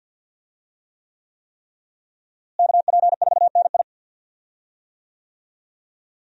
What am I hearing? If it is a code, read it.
Morse code, heard as a keyed tone, about 46 wpm, 700 Hz. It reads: KP4NI
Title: USA Call Signs - 5 Characters - Repeated 3x - 50wpm